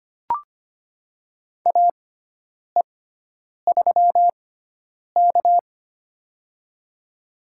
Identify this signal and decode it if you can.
Morse code, heard as a keyed tone, about 25 wpm, spaced as for 9 wpm, 700 Hz. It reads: AE3K
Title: USA Call Signs - 4 Characters - 6x Character Spacing - 25wpm